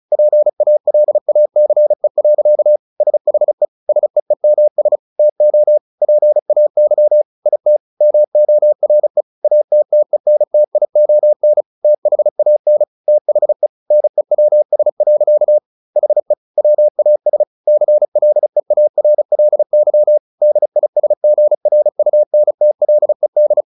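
Morse code, heard as a keyed tone, about 34 wpm, 600 Hz. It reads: PALACE. SHE SEEMS TO PAY IT MORE ATTENTION THAN THE NEWS. HE WAS CLEARLY DISGRUNTLED
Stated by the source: Morse practice at 35 wpm from CW med Gnister, sending Ken Follett's novel Never in English.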